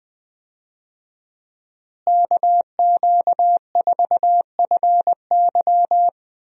Morse code, heard as a keyed tone, about 20 wpm, 700 Hz. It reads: KQ4FY
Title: USA Call Signs - 5 Characters - Repeated 2x - 20wpm